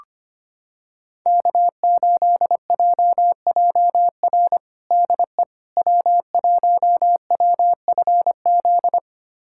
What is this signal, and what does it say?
Morse code, heard as a keyed tone, about 25 wpm, 700 Hz. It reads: K8JJR DE W1WFZ
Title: USA Callsign DE Callsign - 5 Characters - 25wpm